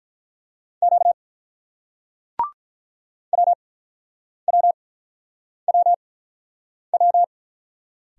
Morse code, heard as a keyed tone, about 48 wpm, spaced as for 9 wpm, 700 Hz. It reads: Q WWWW